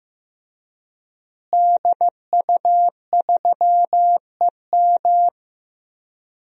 Morse code, heard as a keyed tone, about 15 wpm, 700 Hz. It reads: DU3EM
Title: DXCC-100 Call Signs - 5 Characters - Repeated 3x - 15wpm